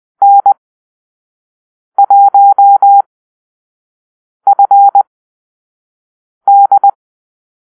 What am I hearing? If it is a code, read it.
Morse code, heard as a keyed tone, about 20 wpm, spaced as for 5 wpm, 800 Hz. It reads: N1FD
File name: Call-Sign Drill